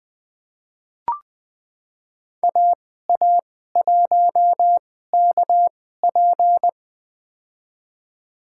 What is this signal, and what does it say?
Morse code, heard as a keyed tone, about 20 wpm, spaced as for 14 wpm, 700 Hz. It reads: AA1KP
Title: USA Call Signs - 5 Characters - 2x Character Spacing - 20wpm